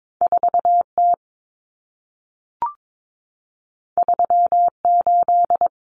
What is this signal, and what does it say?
Morse code, heard as a keyed tone, about 22 wpm, 700 Hz. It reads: 4T 38
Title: Common Cut Numbers - 2-Digit 22wpm